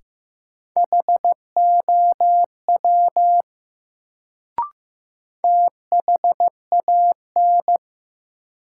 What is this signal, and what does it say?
Morse code, heard as a keyed tone, about 15 wpm, 700 Hz. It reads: HOW THAN
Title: Sentences from Top 100 Words - New Word Drill - 15wpm